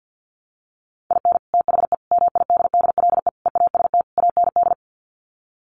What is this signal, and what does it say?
Morse code, heard as a keyed tone, about 50 wpm, 700 Hz. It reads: IN THE MIDDLE EAST AND
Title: Sets of 5 Words Encore 50wpm